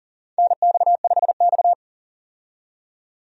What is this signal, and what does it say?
Morse code, heard as a keyed tone, about 40 wpm, 700 Hz. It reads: NX5X
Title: USA Call Signs - 4 Characters - 40wpm